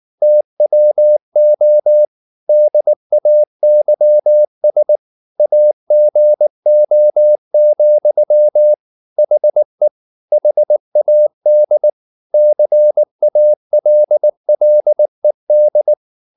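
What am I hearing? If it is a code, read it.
Morse code, heard as a keyed tone, about 19 wpm, 600 Hz. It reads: TWO DAYS AGO, HE HAD CALLED
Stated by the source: Morse code at 19 wpm from CW med Gnister, reading Ken Follett's novel Never in English.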